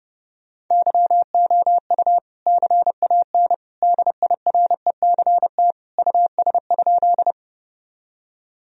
Morse code, heard as a keyed tone, about 29 wpm, 700 Hz. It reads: YOU CAN DIRECT US?
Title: Sentences from Top 1000 Words - Review Part 2 - 30wpm